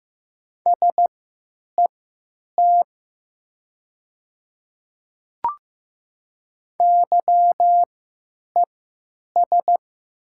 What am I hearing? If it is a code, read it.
Morse code, heard as a keyed tone, about 15 wpm, spaced as for 9 wpm, 700 Hz. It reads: SET YES
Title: 3x Character Spacing - Top 500 Words - 15wpm